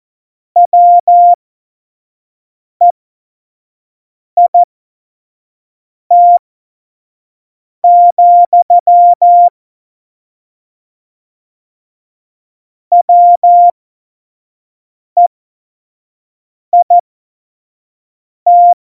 Morse code, heard as a keyed tone, about 14 wpm, spaced as for 5 wpm, 700 Hz. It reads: WEIT, WEIT
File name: Morsekurs / Witze / Witz 28 April 2020, 25 F70